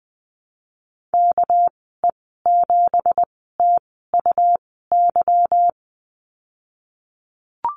Morse code, heard as a keyed tone, about 20 wpm, spaced as for 14 wpm, 700 Hz. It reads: KE7TUY
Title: 213 Lesson 34g - Number 8 - Call Signs 20wpm